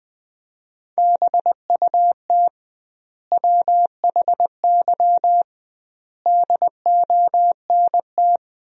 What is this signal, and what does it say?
Morse code, heard as a keyed tone, about 19 wpm, 700 Hz. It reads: BUT WHY DONT
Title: 2x Word-Spacing - Sentences from Top 300 Words - 20wpm